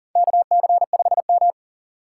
Morse code, heard as a keyed tone, about 40 wpm, 700 Hz. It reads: KC5M
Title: USA Call Signs - 6x Character Spacing - 40wpm